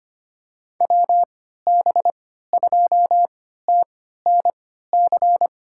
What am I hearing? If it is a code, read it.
Morse code, heard as a keyed tone, about 25 wpm, spaced as for 14 wpm, 700 Hz. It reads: WB2TNC
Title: USA Call Signs - 6 Characters - 3x Character Spacing - 25wpm